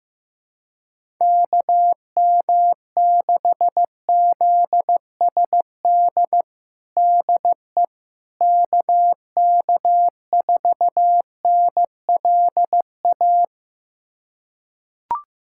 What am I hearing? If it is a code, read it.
Morse code, heard as a keyed tone, about 15 wpm, 700 Hz. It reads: KM6ZSD DE KK4NLA
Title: USA Callsign DE Callsign - 6 Characters - 15wpm